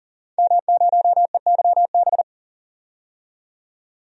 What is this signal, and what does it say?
Morse code, heard as a keyed tone, about 40 wpm, 700 Hz. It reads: M0EYB
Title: DXCC-100 Call Signs - 5 Characters - 40wpm